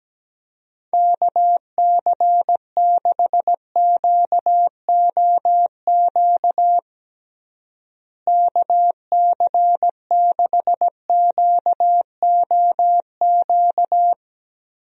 Morse code, heard as a keyed tone, about 17 wpm, 700 Hz. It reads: KC6QOQ KC6QOQ
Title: USA Call Signs - 6 Characters - Repeated 3x - 17wpm